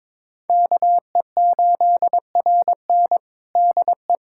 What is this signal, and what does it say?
Morse code, heard as a keyed tone, about 22 wpm, 700 Hz. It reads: KE8RN DE
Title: USA Callsign DE Callsign - 5 Characters - 22wpm